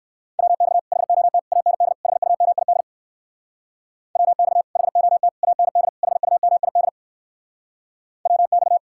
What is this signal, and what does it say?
Morse code, heard as a keyed tone, about 59 wpm, 700 Hz. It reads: WX HOT AND HUMID WX HOT AND HUMID WX